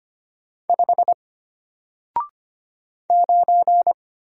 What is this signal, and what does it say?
Morse code, heard as a keyed tone, about 25 wpm, 700 Hz. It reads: 5 9
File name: Single-Letter-Number-25wpm